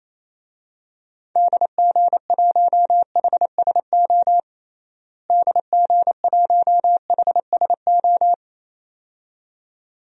Morse code, heard as a keyed tone, about 28 wpm, 700 Hz. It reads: DG1HSO DG1HSO
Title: DXCC-100 Call Signs - 6 Characters - Repeated 2x - 28wpm